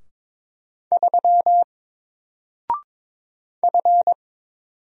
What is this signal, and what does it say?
Morse code, heard as a keyed tone, about 22 wpm, 700 Hz. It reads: 3 F
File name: Single-Letter-Number-22wpm